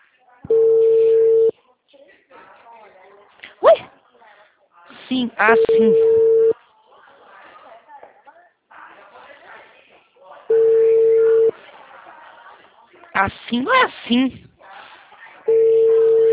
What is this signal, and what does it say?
Ringback tone (China, 450 Hz), with speech.
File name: out- -2010- - - .40112